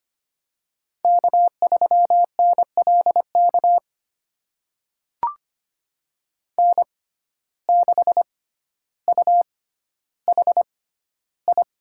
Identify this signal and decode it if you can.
Morse code, heard as a keyed tone, about 25 wpm, spaced as for 13 wpm, 700 Hz. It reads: K3NLK N6UHI
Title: USA Call Signs - 6x Character Spacing - 25wpm